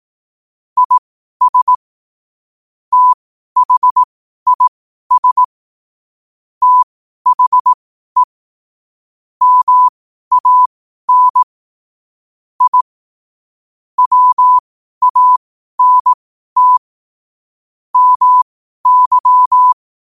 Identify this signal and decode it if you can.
Morse code, heard as a keyed tone, about 18 wpm, spaced as for 12 wpm, 1000 Hz. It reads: IS THIS THE MAN I WANT MY
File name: qotdfiles / 12wpm